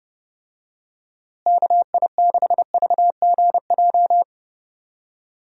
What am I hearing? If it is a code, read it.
Morse code, heard as a keyed tone, about 32 wpm, 700 Hz. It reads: KI6VGJ
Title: USA Call Signs - 6 Characters - Repeated 2x - 30wpm